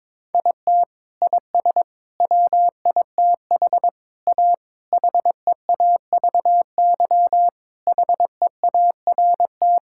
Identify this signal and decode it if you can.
Morse code, heard as a keyed tone, about 22 wpm, 700 Hz. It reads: IT IS WITH A HEAVY HEART
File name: practice / 22wpm / Sets of 8 Words - 22wpm